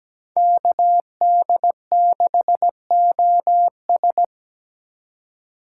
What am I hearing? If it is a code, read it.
Morse code, heard as a keyed tone, about 20 wpm, 700 Hz. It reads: KD6OS
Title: Speed Racing - USA 2x2 Call Signs 17wpm